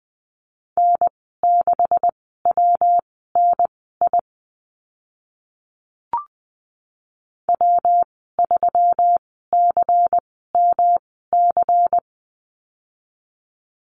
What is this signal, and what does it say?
Morse code, heard as a keyed tone, about 20 wpm, spaced as for 14 wpm, 700 Hz. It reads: N6WNI W3CMC
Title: 128 Lesson 21g - Number 6 - Call Signs 20wpm